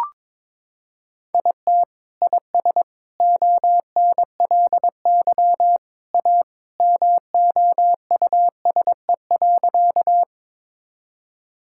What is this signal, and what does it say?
Morse code, heard as a keyed tone, about 22 wpm, 700 Hz. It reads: IT IS ONLY A MOUSE.
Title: Sentences from Top 1000 Words - Review Part 5 - 22wpm